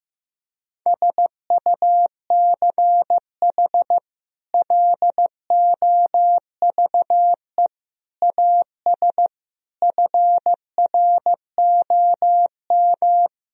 Morse code, heard as a keyed tone, about 15 wpm, 700 Hz. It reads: SUCH LOVE AS FROM
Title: Sentences from Top 1000 Words - Review Part 1 - 15wpm